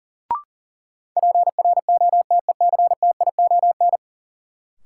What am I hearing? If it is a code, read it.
Morse code, heard as a keyed tone, about 37 wpm, 700 Hz. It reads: PROTECTION
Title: Sentences from Top 1600 Words - New Word Drill Alt - 40wpm